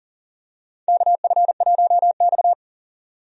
Morse code, heard as a keyed tone, about 38 wpm, 700 Hz. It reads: KF1X
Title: USA Call Signs - 4 Characters - 40wpm